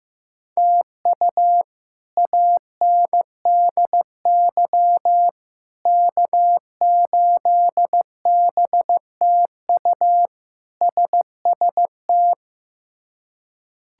Morse code, heard as a keyed tone, about 16 wpm, 700 Hz. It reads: TU ANDY K8BTU SST
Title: K1USN SST QSO Practice - 15wpm